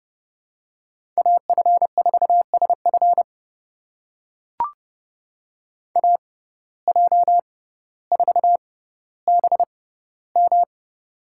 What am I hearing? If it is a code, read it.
Morse code, heard as a keyed tone, about 29 wpm, spaced as for 15 wpm, 700 Hz. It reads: AF4SF AJ4BM
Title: USA Call Signs - 5 Characters - 6x Character Spacing - 30wpm